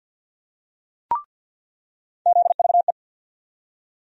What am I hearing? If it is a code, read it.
Morse code, heard as a keyed tone, about 47 wpm, 700 Hz. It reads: ZVE
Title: USA Call Sign Postfixes - 3 Letters 50wpm